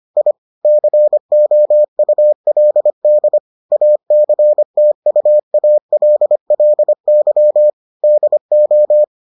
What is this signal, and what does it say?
Morse code, heard as a keyed tone, about 25 wpm, 600 Hz. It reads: I COULD ACTUALLY DO